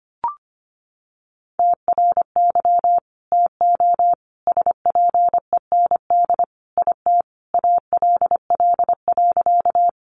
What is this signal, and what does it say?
Morse code, heard as a keyed tone, about 25 wpm, 700 Hz. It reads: TRY TO SPEND IT ALL.
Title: Sentences from Top 800 Words 25wpm